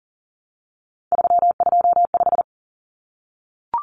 Morse code, heard as a keyed tone, about 40 wpm, 700 Hz. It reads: 325
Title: Numbers - 3-Digit 40wpm